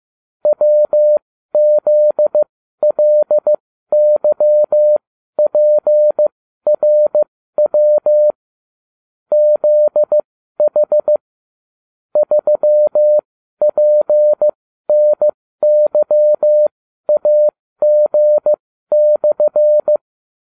Morse code, heard as a keyed tone, about 15 wpm, spaced as for 12 wpm, 600 Hz. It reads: WZLYPRW ZH 3PNYAG/